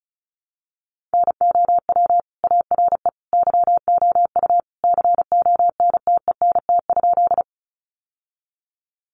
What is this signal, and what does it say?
Morse code, heard as a keyed tone, about 34 wpm, 700 Hz. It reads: NOW ARE YOU CONTENT?
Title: Sentences from Top 900 Words 35wpm